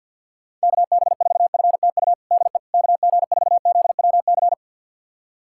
Morse code, heard as a keyed tone, about 52 wpm, 690 Hz. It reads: KB4FTU DE KG4ZWC
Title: USA Callsign DE Callsign - 6 Characters - 50wpm